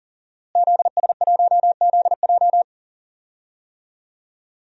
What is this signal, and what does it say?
Morse code, heard as a keyed tone, about 37 wpm, 700 Hz. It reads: ZS1ZJ
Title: DXCC-100 Call Signs - 5 Characters - Repeated 3x - 40wpm